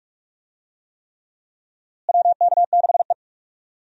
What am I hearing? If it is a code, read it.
Morse code, heard as a keyed tone, about 42 wpm, 700 Hz. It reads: WK6E
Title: USA Call Signs - 4 Characters - Repeated 3x - 45wpm